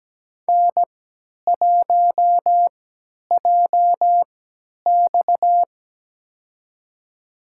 Morse code, heard as a keyed tone, about 17 wpm, spaced as for 10 wpm, 700 Hz. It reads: N1JX
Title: USA Call Signs - 4 Characters - 3x Character Spacing - 17wpm